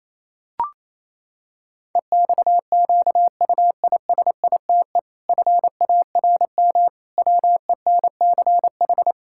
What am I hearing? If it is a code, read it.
Morse code, heard as a keyed tone, about 28 wpm, 700 Hz. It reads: EXQUISITE FARM WENCH
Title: Pangram Challenge v2 - 28wpm